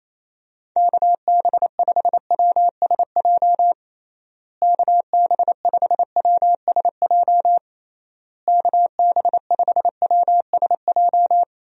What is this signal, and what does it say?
Morse code, heard as a keyed tone, about 28 wpm, 700 Hz. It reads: KB5WSJ KB5WSJ KB5WSJ
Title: USA Call Signs - 6 Characters - Repeated 3x - 28wpm